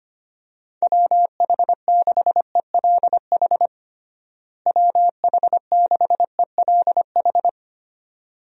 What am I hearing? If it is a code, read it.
Morse code, heard as a keyed tone, about 25 wpm, 700 Hz. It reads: WH6ELH WH6ELH
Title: USA Call Signs - 6 Characters - Repeated 2x - 25wpm